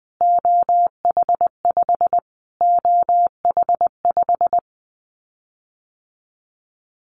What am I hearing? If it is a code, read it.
Morse code, heard as a keyed tone, about 19 wpm, 700 Hz. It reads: OH5 OH5
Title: Callsign Prefixes for Top 25 DXCC - Repeated 3x - 20wpm